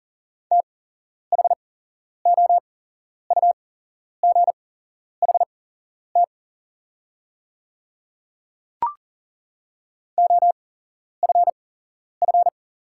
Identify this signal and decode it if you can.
Morse code, heard as a keyed tone, about 40 wpm, spaced as for 11 wpm, 690 Hz. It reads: THOUGHT OFF
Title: 8x Character Spacing - Top 300 Words - 40wpm